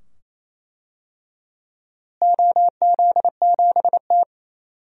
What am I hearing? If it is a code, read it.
Morse code, heard as a keyed tone, about 27 wpm, 700 Hz. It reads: OZ7T
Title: DXCC-100 Call Signs - 4 Characters - Repeated 2x - 28wpm